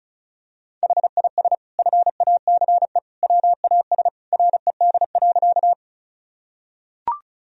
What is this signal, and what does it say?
Morse code, heard as a keyed tone, about 36 wpm, 700 Hz. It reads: HIS FACE WAS RED.
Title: Sentences from Top 1000 Words - Review Part 2 - 35wpm